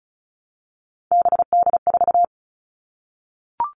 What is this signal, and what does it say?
Morse code, heard as a keyed tone, about 34 wpm, 700 Hz. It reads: BD4
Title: Callsign Prefixes for Top 100 DXCC - 35wpm